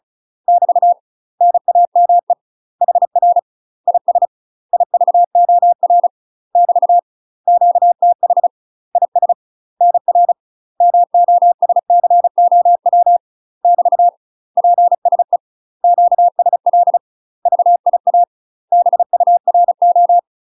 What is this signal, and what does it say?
Morse code, heard as a keyed tone, about 36 wpm, 700 Hz. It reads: <BT> NAME HR IS IVOR <BT> QTH IS NR MOSCOW <BT> PSE QSL VIA BURO
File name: UK MASTER QSOs TRAINING / 35WPM / UK MASTER QSOs 0002